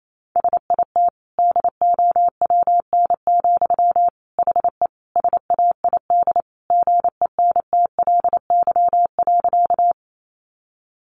Morse code, heard as a keyed tone, about 28 wpm, 700 Hz. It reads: SIT DOWN, HE SAID GENTLY.
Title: Sentences from Top 1500 Words 28wpm